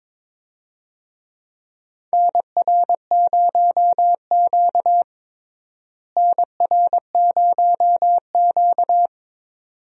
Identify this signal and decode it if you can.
Morse code, heard as a keyed tone, about 22 wpm, 700 Hz. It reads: NR0Q NR0Q
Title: USA Call Signs - 4 Characters - Repeated 2x - 22wpm